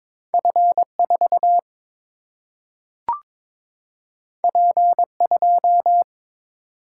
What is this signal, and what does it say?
Morse code, heard as a keyed tone, about 22 wpm, 700 Hz. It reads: F 4 P 2
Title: Single-Letter Single-Number 22wpm